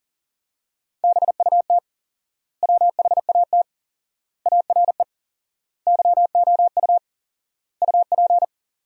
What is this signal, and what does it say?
Morse code, heard as a keyed tone, about 37 wpm, 700 Hz. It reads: BUT WHAT ARE YOU UP
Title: 4x Word-Spacing - Sentences from Top 100 Words - 40wpm